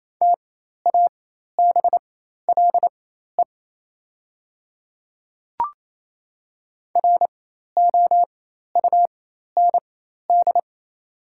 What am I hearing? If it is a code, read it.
Morse code, heard as a keyed tone, about 28 wpm, spaced as for 13 wpm, 700 Hz. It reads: TABLE ROUND
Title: 4x Character Spacing - Top 500 Words - 28wpm